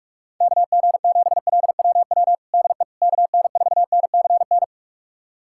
Morse code, heard as a keyed tone, about 46 wpm, 690 Hz. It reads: KG7LWW DE KN4NCN
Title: USA Callsign DE Callsign 45wpm